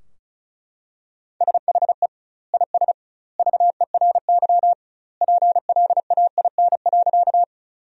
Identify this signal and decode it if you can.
Morse code, heard as a keyed tone, about 35 wpm, 700 Hz. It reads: SHE IS VERY PLAIN.
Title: 2x Word-Spacing - Sentences from Top 400 Words - 35wpm